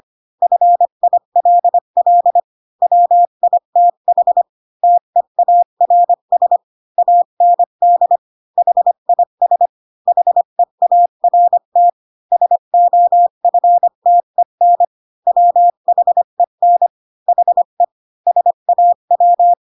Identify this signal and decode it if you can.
Morse code, heard as a keyed tone, about 25 wpm, 700 Hz. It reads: FILL WITH TEARS AND HIS HEART SOFTEN WHEN HE SAW
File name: Pinocchio / Pinocchio ch8 20 25 ik1hge